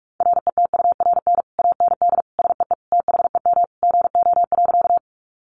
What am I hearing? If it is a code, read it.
Morse code, heard as a keyed tone, about 44 wpm, 700 Hz. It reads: RETURN AND SEE THEM GO.
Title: Sentences from Top 800 Words 45wpm